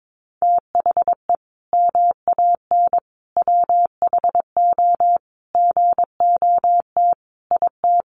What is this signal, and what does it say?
Morse code, heard as a keyed tone, about 22 wpm, 700 Hz. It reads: THE MAN WHO GOT IT
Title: Sentences from Top 1400 Words 22wpm